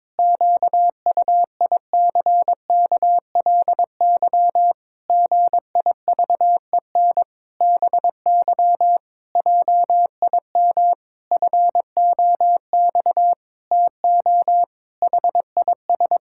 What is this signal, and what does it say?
Morse code, heard as a keyed tone, about 22 wpm, 700 Hz. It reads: QUICKLY GIVEN BY JIM FOX TO HIS
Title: Pangram Challenge v2 - 22wpm